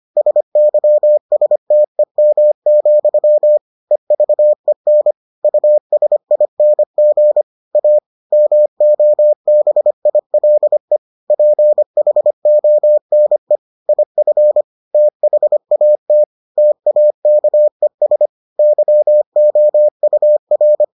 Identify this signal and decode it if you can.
Morse code, heard as a keyed tone, about 25 wpm, 600 Hz. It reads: SYSTEM, EVEN USING A MOBILE PHONE IF THAT TAKES YOUR